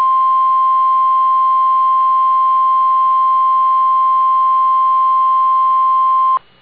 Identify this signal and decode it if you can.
Emergency Alert System (SAME) tones (the 1050 Hz attention tone). It is heard from 1.0 to 6.2 s.